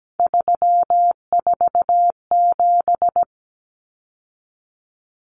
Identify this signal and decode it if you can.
Morse code, heard as a keyed tone, about 17 wpm, 700 Hz. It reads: 347
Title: Numbers - 3-Digit 17wpm